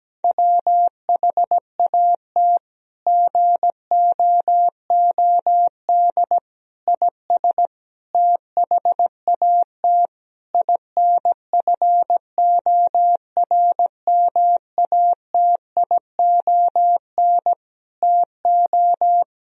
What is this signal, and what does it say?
Morse code, heard as a keyed tone, about 17 wpm, 700 Hz. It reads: WHAT GOOD IS THAT INFORMATION TO
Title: Sentences from Top 1000 Words - Review Part 3 - 17wpm